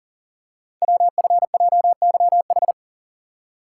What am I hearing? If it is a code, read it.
Morse code, heard as a keyed tone, about 42 wpm, 700 Hz. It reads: WFJYH
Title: Code Groups - 5 Letters - 40wpm